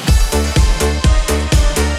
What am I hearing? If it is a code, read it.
There is no code.